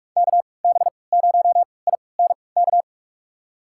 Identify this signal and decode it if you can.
Morse code, heard as a keyed tone, about 43 wpm, spaced as for 25 wpm, 700 Hz. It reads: KB0INK
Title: USA Call Signs - 6 Characters - 3x Character Spacing - 45wpm